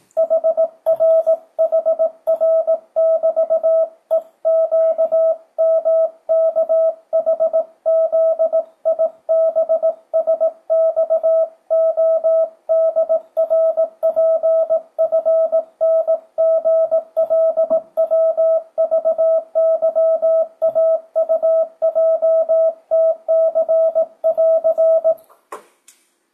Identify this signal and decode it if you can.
Morse code, heard as a keyed tone, about 18 wpm, 650 Hz. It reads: HRHR<BT>EQMKHZIBSXODRPFNGLWVYAUJTC<AR>
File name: 2025cw / speedup / 18wpm